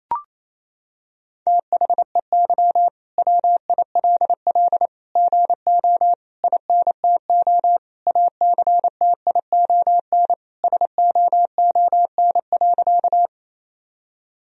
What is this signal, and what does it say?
Morse code, heard as a keyed tone, about 27 wpm, 700 Hz. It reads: THEY WILL GO INTO ACTION SOON.
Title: Sentences from Top 1000 Words - Review Part 4 - 28wpm